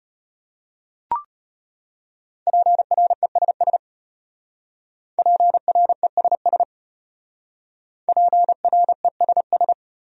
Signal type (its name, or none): Morse code